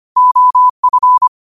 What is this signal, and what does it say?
Morse code, heard as a keyed tone, about 25 wpm, 1000 Hz. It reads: OF
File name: qotdfiles / 25wpm